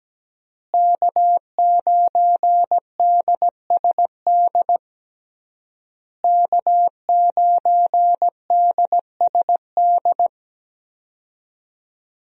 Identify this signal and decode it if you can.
Morse code, heard as a keyed tone, about 17 wpm, 700 Hz. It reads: K9DSD K9DSD
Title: USA Call Signs - 5 Characters - Repeated 3x - 17wpm